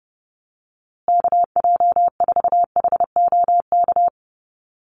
Morse code, heard as a keyed tone, about 30 wpm, 700 Hz. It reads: KJ4HOK
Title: USA Call Signs Repeated 2x 30wpm